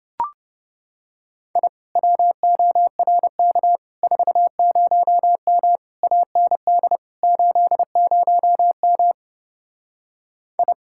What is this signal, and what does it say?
Morse code, heard as a keyed tone, about 28 wpm, 700 Hz. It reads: I WORK 40M AND 80M I